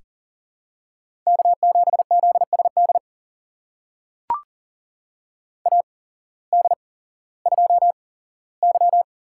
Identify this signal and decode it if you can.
Morse code, heard as a keyed tone, about 40 wpm, spaced as for 19 wpm, 700 Hz. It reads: K7ZSD AD2Y
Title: USA Call Signs - 8x Character Spacing - 40wpm